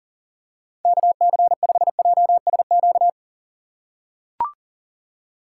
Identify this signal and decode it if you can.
Morse code, heard as a keyed tone, about 40 wpm, 700 Hz. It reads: KC5JSQ
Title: USA Call Signs - 8x Character Spacing - 40wpm